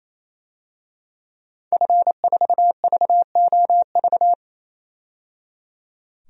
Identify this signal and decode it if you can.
Morse code, heard as a keyed tone, about 27 wpm, 700 Hz. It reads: F4VOV
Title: DXCC-100 Call Signs - 5 Characters - 28wpm